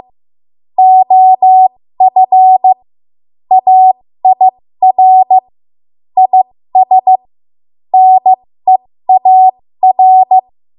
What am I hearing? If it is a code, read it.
Morse code, heard as a keyed tone, about 15 wpm, spaced as for 13 wpm, 750 Hz. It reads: OF AIR IS NEAR
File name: Morse / 13WPM